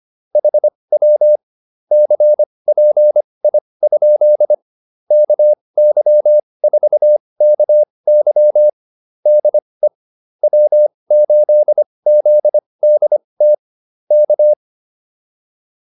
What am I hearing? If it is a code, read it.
Morse code, heard as a keyed tone, about 25 wpm, spaced as for 20 wpm, 600 Hz. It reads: HW CPI? KY4KY DE W8ZDT K